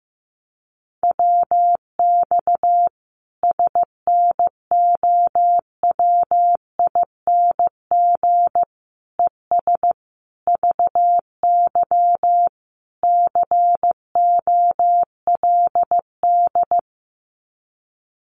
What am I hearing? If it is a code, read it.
Morse code, heard as a keyed tone, about 15 wpm, 700 Hz. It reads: WX SNOWING ES VY COLD
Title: Common QSO Phrases 15wpm